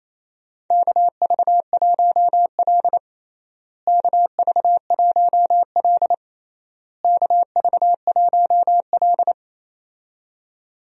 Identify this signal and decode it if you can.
Morse code, heard as a keyed tone, about 28 wpm, 700 Hz. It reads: KV1L KV1L KV1L
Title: USA Call Signs - 4 Characters - Repeated 3x - 28wpm